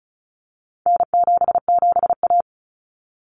Morse code, heard as a keyed tone, about 34 wpm, 700 Hz. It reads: N77A
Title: Common Cut Numbers - 4-Digit 35wpm